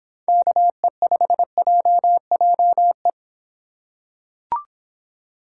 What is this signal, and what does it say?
Morse code, heard as a keyed tone, about 26 wpm, 700 Hz. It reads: KE5JJE